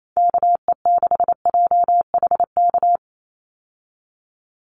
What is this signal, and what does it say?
Morse code, heard as a keyed tone, about 28 wpm, 700 Hz. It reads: KE6JHK
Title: USA Call Signs 28wpm